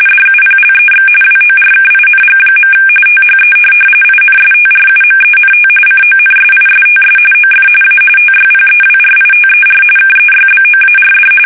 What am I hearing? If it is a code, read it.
RTTY, about 45.45 baud. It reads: DUPLEX WORKING. THE INSTRUMENT CONSISTS OF A TYPEWRITER KEYBOARD